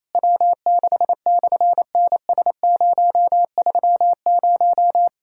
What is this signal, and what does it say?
Morse code, heard as a keyed tone, about 27 wpm, 700 Hz. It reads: W6/NS030
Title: SOTA QSO Practice 28wpm